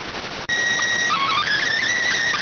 SSTV (slow-scan television), PD 290.